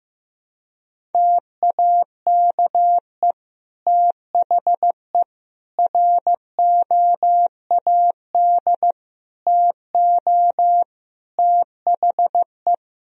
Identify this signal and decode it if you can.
Morse code, heard as a keyed tone, about 15 wpm, 700 Hz. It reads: TAKE THE ROAD TO THE
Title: Sentences from Top 400 Words and Abbreviations 15wpm